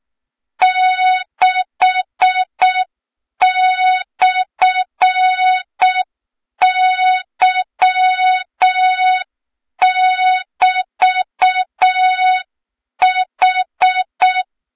Morse code, heard as a keyed tone, about 6 wpm, 3700 Hz. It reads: 6/Y<BT>H
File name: Morze-HA2MN / hang / 23 GYAKORLAT